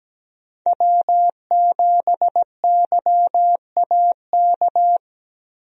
Morse code, heard as a keyed tone, about 17 wpm, 700 Hz. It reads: W7YAK
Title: USA Call Signs - 5 Characters - Repeated 3x - 17wpm